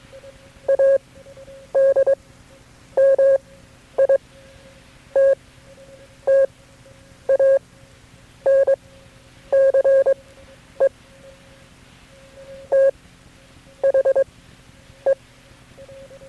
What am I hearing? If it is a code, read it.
Morse code, heard as a keyed tone, about 22 wpm, spaced as for 9 wpm, 540 Hz. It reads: ADMITTANCE THE
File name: KB6MT-tape3-side1